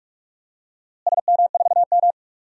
Morse code, heard as a keyed tone, about 47 wpm, 700 Hz. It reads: SM4M
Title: DXCC-100 Call Signs - 4 Characters - Repeated 2x - 45wpm